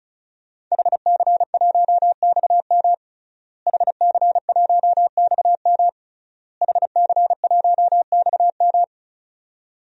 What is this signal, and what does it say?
Morse code, heard as a keyed tone, about 36 wpm, 700 Hz. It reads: HC1XM HC1XM HC1XM
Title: DXCC-100 Call Signs - 5 Characters - Repeated 3x - 35wpm